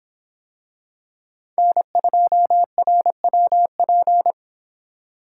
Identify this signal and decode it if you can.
Morse code, heard as a keyed tone, about 24 wpm, 700 Hz. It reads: N2RWP